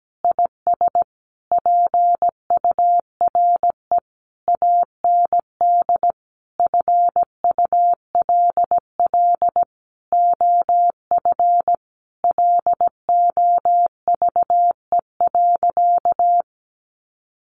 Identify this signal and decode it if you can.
Morse code, heard as a keyed tone, about 17 wpm, 700 Hz. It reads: IS PURE AND FULL OF LOVE.
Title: Sentences from Top 1500 Words 17wpm